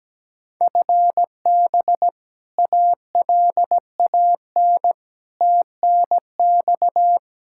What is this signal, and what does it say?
Morse code, heard as a keyed tone, about 17 wpm, 700 Hz. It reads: FB ALAN TNX
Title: Practice QSO - Exchange 3 - More Info - 17wpm